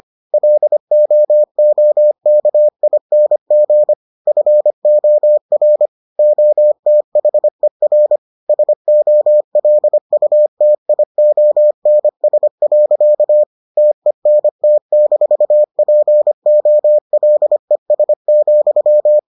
Morse code, heard as a keyed tone, about 25 wpm, 600 Hz. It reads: LOOKING FOR OTHER SOLUTIONS. TENT-POLES,